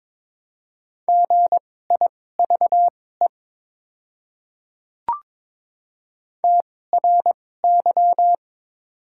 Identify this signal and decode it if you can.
Morse code, heard as a keyed tone, about 21 wpm, spaced as for 16 wpm, 700 Hz. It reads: GIVE TRY
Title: 2x Character Spacing - Top 300 Words - 22wpm